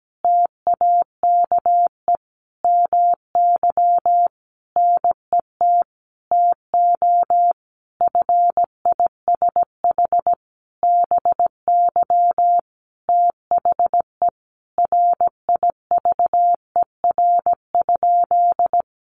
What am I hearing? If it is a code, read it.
Morse code, heard as a keyed tone, about 17 wpm, 700 Hz. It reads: TAKE MY NET TO FISH BY THE RIVER?
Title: Sentences from Top 1400 Words 17wpm